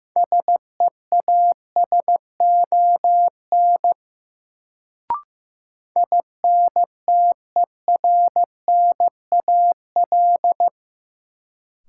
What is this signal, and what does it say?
Morse code, heard as a keyed tone, about 15 wpm, 700 Hz. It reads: SEASON INTERNAL
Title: Sentences from Top 1400 Words - New Word Drill Alt - 15wpm